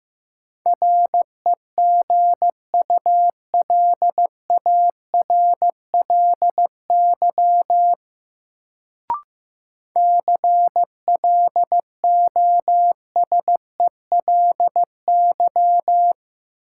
Morse code, heard as a keyed tone, about 15 wpm, 700 Hz. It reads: REGULARLY CLOSELY
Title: Sentences from Top 1400 Words - New Word Drill - 15wpm